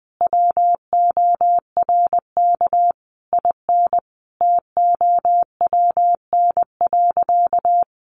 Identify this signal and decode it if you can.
Morse code, heard as a keyed tone, about 20 wpm, 700 Hz. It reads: WORK IN TOWN.
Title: Sentences from Top 1500 Words 20wpm